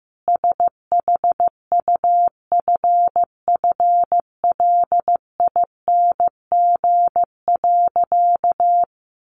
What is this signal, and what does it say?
Morse code, heard as a keyed tone, about 15 wpm, 700 Hz. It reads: SHUFFLING.
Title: A Sound Of Thunder 15wpm